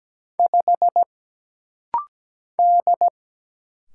Morse code, heard as a keyed tone, about 17 wpm, 700 Hz. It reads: 5 D
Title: Single Letter-Number - Rapid-Fire 17wpm